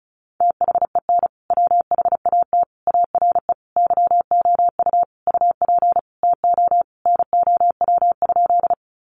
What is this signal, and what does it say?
Morse code, heard as a keyed tone, about 34 wpm, 700 Hz. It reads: THEN WHAT ARE YOU UP TO NOW?
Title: Sentences from Top 100 Words 35wpm